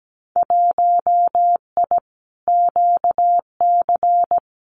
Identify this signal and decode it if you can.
Morse code, heard as a keyed tone, about 17 wpm, 700 Hz. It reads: 1I QC
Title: Winter Field Day Exchanges 17wpm